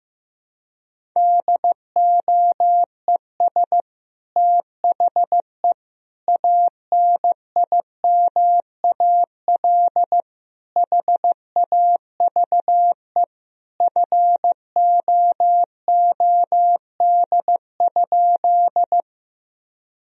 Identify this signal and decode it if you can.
Morse code, heard as a keyed tone, about 15 wpm, 700 Hz. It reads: DOES THE ANIMAL HAVE FOOD?
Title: Sentences from Top 1000 Words - Review Part 1 - 15wpm